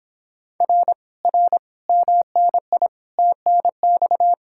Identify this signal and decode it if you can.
Morse code, heard as a keyed tone, about 26 wpm, 700 Hz. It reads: R R MNI TNX